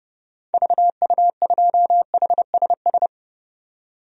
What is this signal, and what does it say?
Morse code, heard as a keyed tone, about 29 wpm, 700 Hz. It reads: VU2HSS
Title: DXCC-100 Call Signs - 6 Characters - 30wpm